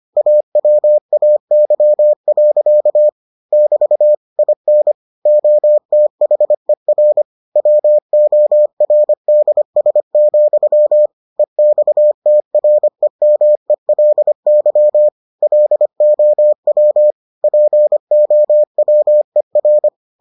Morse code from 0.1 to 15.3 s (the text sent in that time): AWAY. <BT> IN OTHER WORDS, EXTREMELY